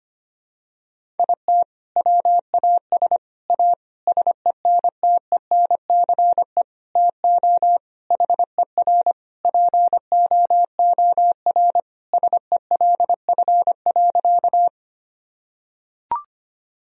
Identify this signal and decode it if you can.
Morse code, heard as a keyed tone, about 25 wpm, 700 Hz. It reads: IT WAS A SENTENCE TO HER POOR SELF.
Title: Sentences from Top 1000 Words - Review Part 1 - 25wpm